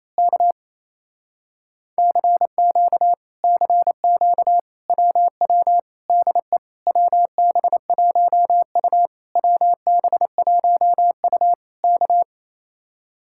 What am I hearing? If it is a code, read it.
Morse code, heard as a keyed tone, about 27 wpm, 700 Hz. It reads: K CQ CQ WW DE WB1U WB1U K
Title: Speed-Racing Common QSO Phrases Encore - 28wpm